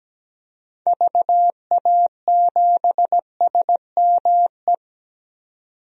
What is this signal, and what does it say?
Morse code, heard as a keyed tone, about 17 wpm, 700 Hz. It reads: VA7SME
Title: DXCC-100 Call Signs - 6 Characters - 17wpm